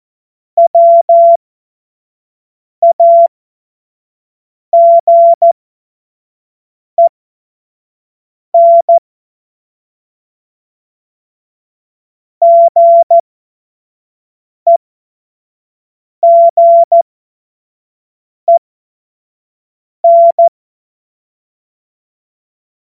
Morse code, heard as a keyed tone, about 14 wpm, spaced as for 5 wpm, 680 Hz. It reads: WAGEN GEGEN
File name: Morsekurs / Witze / Witz 31. Mai 2022 25 F70